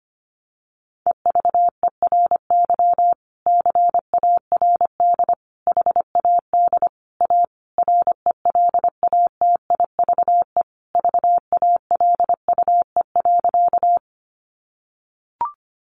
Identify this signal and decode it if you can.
Morse code, heard as a keyed tone, about 25 wpm, 700 Hz. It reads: EVERY CARD HAD A RELATIVE VALUE.
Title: Sentences from Top 1400 Words 25wpm